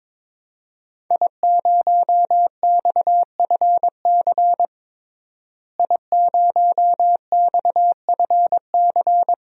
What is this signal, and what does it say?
Morse code, heard as a keyed tone, about 21 wpm, 700 Hz. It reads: I0XFC I0XFC
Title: DXCC-100 Call Signs - 5 Characters - Repeated 3x - 22wpm